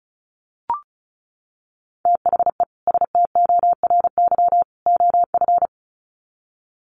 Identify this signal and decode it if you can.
Morse code, heard as a keyed tone, about 34 wpm, 700 Hz. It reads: THE STORY OF
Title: Sets of 3 Words 35wpm